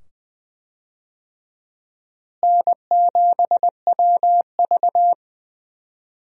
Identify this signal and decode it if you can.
Morse code, heard as a keyed tone, about 20 wpm, 700 Hz. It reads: N7WV
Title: USA Call Signs - 4 Characters - Repeated 2x - 20wpm